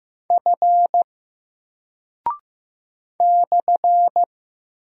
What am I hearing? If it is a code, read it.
Morse code, heard as a keyed tone, about 15 wpm, 700 Hz. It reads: F /
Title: Single Character (Letter:Number:Punctuation:Prosign) 15wpm